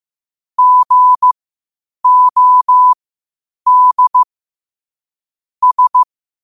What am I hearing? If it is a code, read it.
Morse code, heard as a keyed tone, about 15 wpm, spaced as for 8 wpm, 1000 Hz. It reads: GOD S
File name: qotdfiles / 8wpm